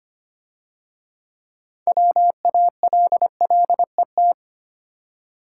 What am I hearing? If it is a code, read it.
Morse code, heard as a keyed tone, about 30 wpm, 700 Hz. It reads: WALLET
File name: Easy Words - Speed-Racing with Hints - 25wpm